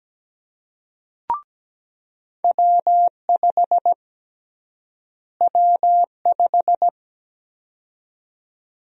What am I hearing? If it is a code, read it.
Morse code, heard as a keyed tone, about 17 wpm, 700 Hz. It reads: W5 W5
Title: USA Call Sign Prefixes - Repeated 2x - 17wpm